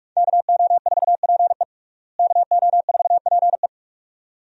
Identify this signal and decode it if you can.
Morse code, heard as a keyed tone, about 46 wpm, 700 Hz. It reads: KO4PE KO4PE
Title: USA Call Signs - 5 Characters - Repeated 2x - 45wpm